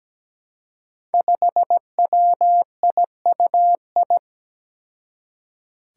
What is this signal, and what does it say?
Morse code, heard as a keyed tone, about 17 wpm, 700 Hz. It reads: 5WIUI
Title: Five Challenging Letter-Number Combinations 17wpm